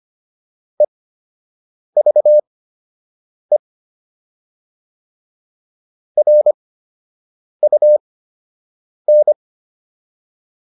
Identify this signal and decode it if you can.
Morse code, heard as a keyed tone, about 25 wpm, spaced as for 7 wpm, 600 Hz. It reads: EVE RUN